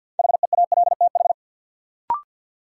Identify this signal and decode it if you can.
Morse code, heard as a keyed tone, about 53 wpm, 700 Hz. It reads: HEALTH
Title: Sentences from Top 600 Words - New Word Drill - 50wpm